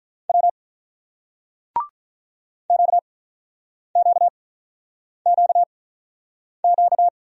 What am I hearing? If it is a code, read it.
Morse code, heard as a keyed tone, about 48 wpm, spaced as for 9 wpm, 700 Hz. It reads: W QQQQ